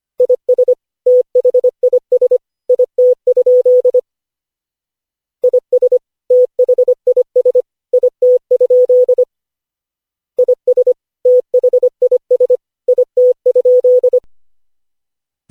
Morse code, heard as a keyed tone, about 25 wpm, 500 Hz. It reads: IS THIS IT? IS THIS IT? IS THIS IT?
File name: cw / Session5 cw training